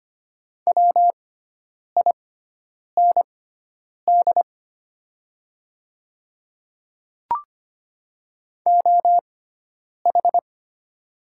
Morse code, heard as a keyed tone, about 25 wpm, spaced as for 9 wpm, 700 Hz. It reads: WIND OH
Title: 6x Character Spacing - Top 500 Words - 25wpm